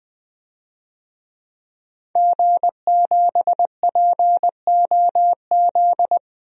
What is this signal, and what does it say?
Morse code, heard as a keyed tone, about 20 wpm, 700 Hz. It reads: G7POZ
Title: DXCC-100 Call Signs - 5 Characters - Repeated 3x - 20wpm